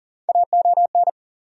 Morse code, heard as a keyed tone, about 40 wpm, 700 Hz. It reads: AON
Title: USA Call Sign Postfixes - 3 Letters 40wpm